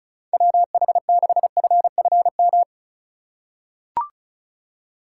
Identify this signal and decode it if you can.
Morse code, heard as a keyed tone, about 36 wpm, 700 Hz. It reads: WH6FFM
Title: USA Call Signs - 6 Characters - 3x Character Spacing - 35wpm